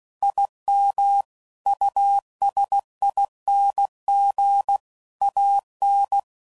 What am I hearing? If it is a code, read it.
Morse code, heard as a keyed tone, about 16 wpm, 800 Hz. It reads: IM USING AN
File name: CW Code Groups @15WPM / QSOs / WD0FBT Spaced